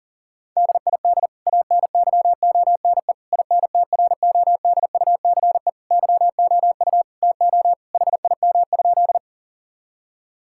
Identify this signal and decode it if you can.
Morse code, heard as a keyed tone, about 40 wpm, 700 Hz. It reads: DID ANYONE INTRODUCE YOU TO HIM?
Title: Sentences from Top 1600 Words 40wpm